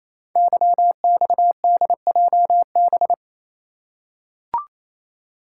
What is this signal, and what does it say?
Morse code, heard as a keyed tone, about 28 wpm, 700 Hz. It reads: YXDJB
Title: Code Groups - 5 Letters - 28wpm